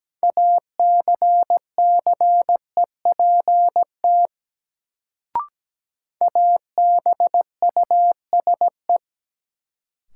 Morse code, heard as a keyed tone, about 17 wpm, 700 Hz. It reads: ACCEPT ABUSE
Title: Sentences from Top 1300 Words - New Word Drill - 17wpm